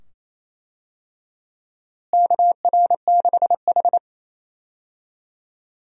Morse code, heard as a keyed tone, about 28 wpm, 700 Hz. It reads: KR6H
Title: USA Call Signs - 4 Characters - Repeated 2x - 28wpm